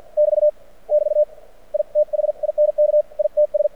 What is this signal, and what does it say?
Morse code, heard as a keyed tone, about 51 wpm, spaced as for 44 wpm, 600 Hz. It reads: <BT> <BT> I THINK ITS
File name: archived / CRSnet